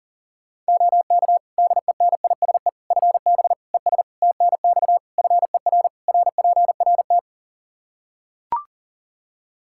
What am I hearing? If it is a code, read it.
Morse code, heard as a keyed tone, about 40 wpm, 700 Hz. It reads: OK DENISE FB ES TNX FER RPRT
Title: K1USN SST QSO Practice 40wpm